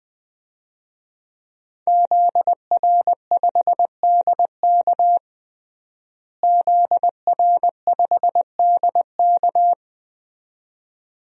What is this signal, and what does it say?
Morse code, heard as a keyed tone, about 20 wpm, 700 Hz. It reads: ZR5DK ZR5DK
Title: DXCC-100 Call Signs - 5 Characters - Repeated 2x - 20wpm